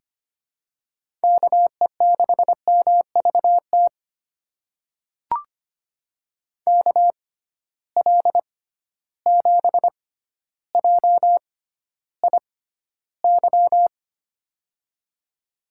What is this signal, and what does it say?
Morse code, heard as a keyed tone, about 25 wpm, spaced as for 13 wpm, 700 Hz. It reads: KE6MVT KL7JIY
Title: USA Call Signs - 6 Characters - 6x Character Spacing - 25wpm